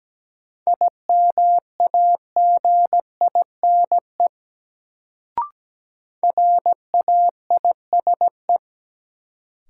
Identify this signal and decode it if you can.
Morse code, heard as a keyed tone, about 17 wpm, 700 Hz. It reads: IMAGINE RAISE
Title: Sentences from Top 1400 Words - New Word Drill - 17wpm